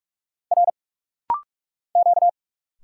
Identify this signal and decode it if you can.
Morse code, heard as a keyed tone, about 46 wpm, 690 Hz. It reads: R Q
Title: Single Letters - Rapid-Fire 45wpm